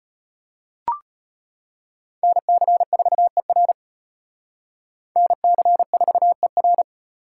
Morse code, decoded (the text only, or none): NC4ER NC4ER